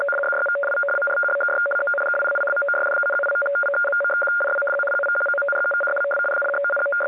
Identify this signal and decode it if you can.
Two-tone teleprinter signal, about 850 Hz shift.